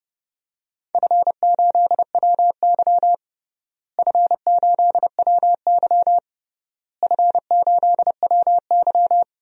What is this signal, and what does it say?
Morse code, heard as a keyed tone, about 30 wpm, 700 Hz. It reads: F8WY F8WY F8WY
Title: DXCC-100 Call Signs - 4 Characters - Repeated 3x - 30wpm